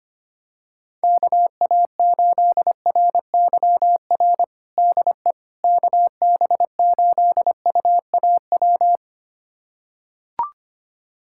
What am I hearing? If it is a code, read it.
Morse code, heard as a keyed tone, about 25 wpm, 700 Hz. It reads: KA8RYR DE KB8UAW
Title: USA Callsign DE Callsign 25wpm